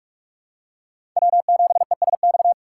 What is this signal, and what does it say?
Morse code, heard as a keyed tone, about 43 wpm, 700 Hz. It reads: W7ESX
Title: USA Call Signs - 6x Character Spacing - 45wpm